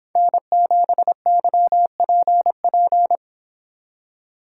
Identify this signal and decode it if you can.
Morse code, heard as a keyed tone, about 24 wpm, 700 Hz. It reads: N7YPP